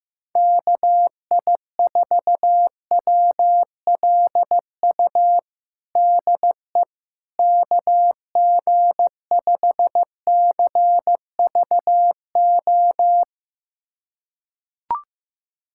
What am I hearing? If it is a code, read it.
Morse code, heard as a keyed tone, about 15 wpm, 700 Hz. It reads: KI4WLU DE KG5CVO
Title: USA Callsign DE Callsign - 6 Characters - 15wpm